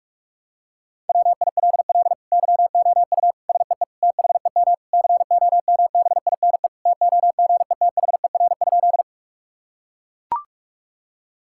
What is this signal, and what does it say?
Morse code, heard as a keyed tone, about 41 wpm, 700 Hz. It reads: WILL YOU SEE THEM COMBINE TOGETHER?
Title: Sentences from Top 1600 Words 45wpm